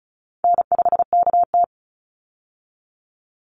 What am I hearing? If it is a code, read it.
Morse code, heard as a keyed tone, about 34 wpm, 700 Hz. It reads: N5KT
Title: Call Signs Repeated 3x - 35wpm